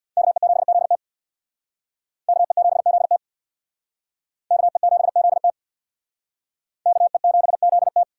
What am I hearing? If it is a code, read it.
Morse code, heard as a keyed tone, about 60 wpm, 700 Hz. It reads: KE7ZT KE7ZT KE7ZT KE7ZT
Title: Speed Racing - USA Call Signs 50wpm